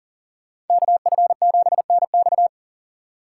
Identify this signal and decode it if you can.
Morse code, heard as a keyed tone, about 51 wpm, 700 Hz. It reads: KF7NX